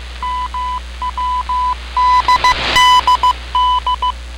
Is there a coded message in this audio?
Yes: Morse code from 0.2 to 4.1 s, heard as a keyed tone, about 15 wpm, 1000 Hz. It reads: MWDDD